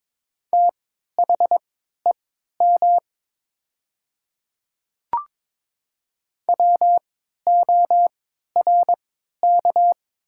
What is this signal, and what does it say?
Morse code, heard as a keyed tone, about 22 wpm, spaced as for 12 wpm, 700 Hz. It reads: THEM WORK
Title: 3x Character Spacing - Top 200 Words - 22wpm